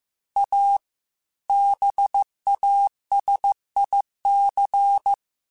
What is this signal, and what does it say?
Morse code, heard as a keyed tone, about 15 wpm, 800 Hz. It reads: A BASIC
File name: Technical Subjects / Technical Subjects-mp3s / Sw Pwr Sup @13